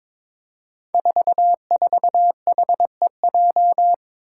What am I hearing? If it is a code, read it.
Morse code, heard as a keyed tone, about 23 wpm, 700 Hz. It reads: 44HEJ
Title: Five Challenging Letter-Number Combinations 22wpm